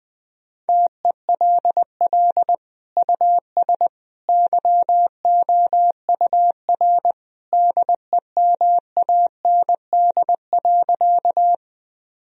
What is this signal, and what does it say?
Morse code, heard as a keyed tone, about 20 wpm, 700 Hz. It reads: TELL US YOUR DEMAND.
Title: Sentences from Top 1000 Words - Review Part 4 - 20wpm